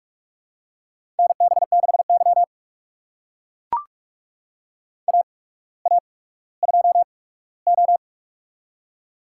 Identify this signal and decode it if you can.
Morse code, heard as a keyed tone, about 43 wpm, spaced as for 19 wpm, 700 Hz. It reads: NB6Y AA2O
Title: USA Call Signs - 4 Characters - 8x Character Spacing - 45wpm